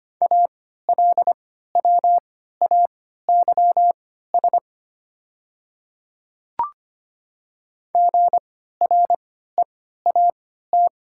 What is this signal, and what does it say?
Morse code, heard as a keyed tone, about 25 wpm, spaced as for 14 wpm, 700 Hz. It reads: ALWAYS GREAT
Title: 3x Character Spacing - Top 500 Words - 25wpm